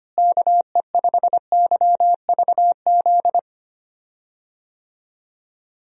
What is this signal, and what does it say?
Morse code, heard as a keyed tone, about 25 wpm, 700 Hz. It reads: KE5YVZ
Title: USA Call Signs - 6 Characters - Repeated 3x - 25wpm